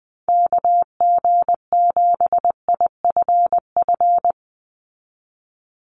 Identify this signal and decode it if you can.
Morse code, heard as a keyed tone, about 20 wpm, 700 Hz. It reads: KG7IFF
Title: USA Call Signs 20wpm